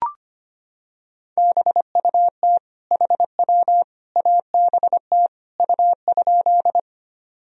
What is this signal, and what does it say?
Morse code, heard as a keyed tone, about 25 wpm, 700 Hz. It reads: BUT HW ABT U?
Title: Sentences from Top 100 Words and Abbreviations 25wpm